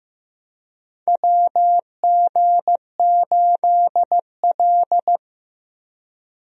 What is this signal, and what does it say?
Morse code, heard as a keyed tone, about 15 wpm, 700 Hz. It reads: WG8L
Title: USA Call Signs - 4 Characters - Repeated 3x - 15wpm